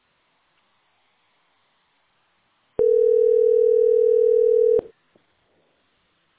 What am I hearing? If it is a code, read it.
Ringback tone (North America), 440 + 480 Hz.